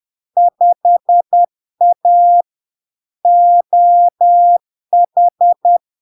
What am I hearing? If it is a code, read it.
Morse code, heard as a keyed tone, about 10 wpm, 700 Hz. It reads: 5A OH
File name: LARK-FD-2022-Log-in-CW-10wpm-700-Hz